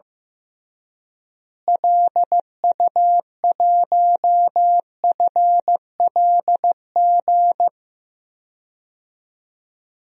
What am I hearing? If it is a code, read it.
Morse code, heard as a keyed tone, about 15 wpm, 700 Hz. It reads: LU1FLG
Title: DXCC-100 Call Signs - 6 Characters - Repeated 3x - 15wpm